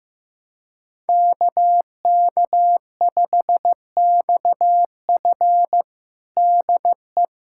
Morse code, heard as a keyed tone, about 15 wpm, 700 Hz. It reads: KK5XF DE
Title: USA Callsign DE Callsign 15wpm